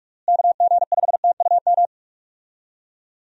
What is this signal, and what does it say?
Morse code, heard as a keyed tone, about 46 wpm, 700 Hz. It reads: KG5TUM
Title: USA Call Signs - 6 Characters - Repeated 3x - 45wpm